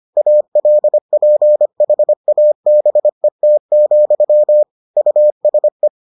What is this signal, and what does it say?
Morse code, heard as a keyed tone, about 25 wpm, 600 Hz. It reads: ALPHABET, USE